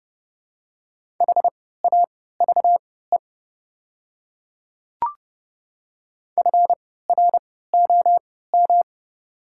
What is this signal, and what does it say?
Morse code, heard as a keyed tone, about 30 wpm, spaced as for 17 wpm, 700 Hz. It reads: HAVE FROM
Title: 3x Character Spacing - Top 100 Words - 30wpm